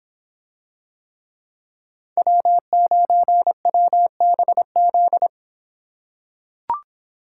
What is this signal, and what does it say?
Morse code, heard as a keyed tone, about 25 wpm, 700 Hz. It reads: W9WBZ